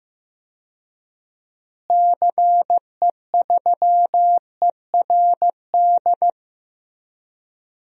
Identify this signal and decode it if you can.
Morse code, heard as a keyed tone, about 15 wpm, 700 Hz. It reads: CE3ERD
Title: DXCC-100 Call Signs - 6 Characters - Repeated 2x - 15wpm